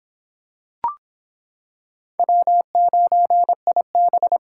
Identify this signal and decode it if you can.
Morse code, heard as a keyed tone, about 26 wpm, 700 Hz. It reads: W9IB